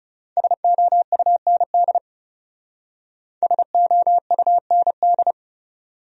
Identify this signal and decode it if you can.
Morse code, heard as a keyed tone, about 37 wpm, 700 Hz. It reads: SOUND SOUND
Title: Speed-Racing Top 100 Words 30wpm